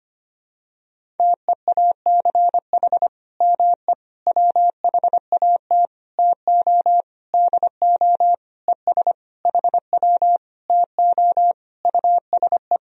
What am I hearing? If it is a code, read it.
Morse code, heard as a keyed tone, about 25 wpm, 700 Hz. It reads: TEACH ME WHAT TO DO ES HW TO USE
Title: Sentences from Top 500 Words and Abbreviations 25wpm